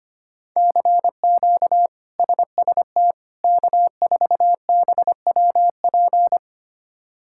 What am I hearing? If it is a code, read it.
Morse code, heard as a keyed tone, about 25 wpm, 700 Hz. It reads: CQ SST K4BWP
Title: K1USN SST QSO Practice - 25wpm